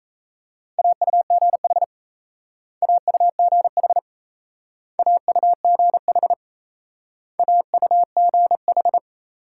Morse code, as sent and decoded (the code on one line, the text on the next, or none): .- ..- --. .... / .- ..- --. .... / .- ..- --. .... / .- ..- --. ....
AUGH AUGH AUGH AUGH